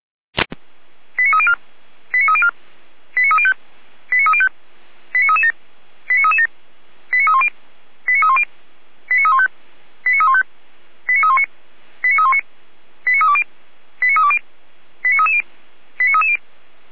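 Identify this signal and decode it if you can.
Five-tone selective call (ZVEI), heard as quick runs of five tones, tones of about 70 ms. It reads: 89304 89304 89305 89305 89337 89337 89310 89310 89315 89315 89319 89319 89320 89320 89330 89330
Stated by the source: ZVEI five-tone selective call received with an SDR.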